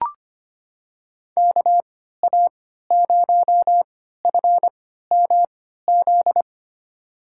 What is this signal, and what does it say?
Morse code, heard as a keyed tone, about 25 wpm, spaced as for 14 wpm, 700 Hz. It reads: KA0FMZ
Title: USA Call Signs - 6 Characters - 3x Character Spacing - 25wpm